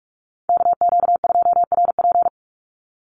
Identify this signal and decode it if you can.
Morse code, heard as a keyed tone, about 44 wpm, 700 Hz. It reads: KQ2RP
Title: NA SOTA Chaser Callsigns 45wpm